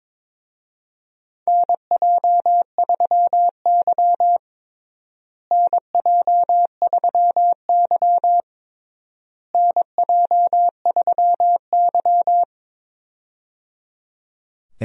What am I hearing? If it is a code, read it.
Morse code, heard as a keyed tone, about 22 wpm, 700 Hz. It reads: NJ3Y NJ3Y NJ3Y
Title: USA Call Signs - 4 Characters - Repeated 3x - 22wpm